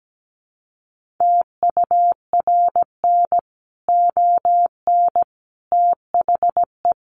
Morse code, heard as a keyed tone, about 17 wpm, 700 Hz. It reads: TURN ON THE
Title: Sentences from Top 1400 Words 17wpm